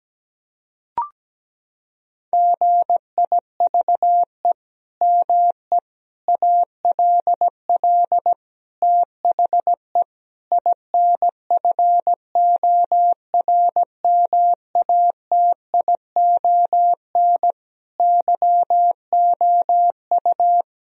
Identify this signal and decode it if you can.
Morse code, heard as a keyed tone, about 17 wpm, 700 Hz. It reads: GIVE ME ALL THE INFORMATION YOU
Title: Sentences from Top 1000 Words - Review Part 3 - 17wpm